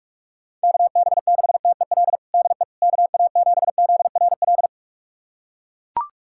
Morse code, heard as a keyed tone, about 41 wpm, 700 Hz. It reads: KB6TEL DE KA7ZRL
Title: USA Callsign DE Callsign 45wpm